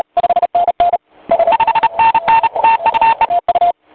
Morse code, heard as a keyed tone, about 32 wpm, 900 Hz, with noise. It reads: 5NN TU E
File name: voice / 15m j3 pa5et 001 2001 08 06